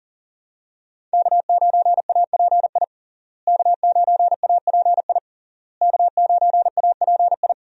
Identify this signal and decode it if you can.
Morse code, heard as a keyed tone, about 39 wpm, 700 Hz. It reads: K9API K9API K9API
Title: USA Call Signs - 5 Characters - Repeated 3x - 40wpm